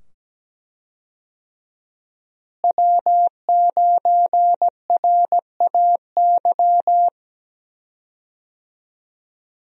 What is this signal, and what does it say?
Morse code, heard as a keyed tone, about 17 wpm, 700 Hz. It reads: W9RAY
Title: USA Call Signs - 5 Characters - Repeated 2x - 17wpm